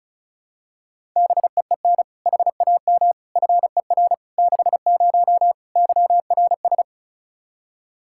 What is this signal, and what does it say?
Morse code, heard as a keyed tone, about 36 wpm, 700 Hz. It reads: BEEN HAM FER 60 YRS
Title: Practice QSO - Exchange 3 - More Info - 35wpm